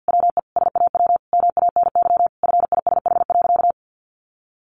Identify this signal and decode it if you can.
Morse code, heard as a keyed tone, about 50 wpm, 700 Hz. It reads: WE SAW MANY FISH.
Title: Sentences from Top 1500 Words 50wpm